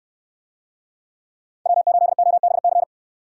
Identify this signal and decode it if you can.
Morse code, heard as a keyed tone, about 66 wpm, 700 Hz. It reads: W8OBC